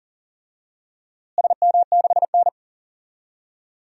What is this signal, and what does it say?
Morse code, heard as a keyed tone, about 40 wpm, 700 Hz. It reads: SM6N
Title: DXCC-100 Call Signs - 4 Characters - Repeated 3x - 40wpm